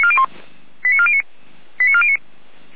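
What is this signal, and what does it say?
Five-tone selective call (ZVEI), heard as quick runs of five tones, tones of about 70 ms. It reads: 89409 89409
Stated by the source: ZVEI five-tone selective call, code 89356, received by SDR.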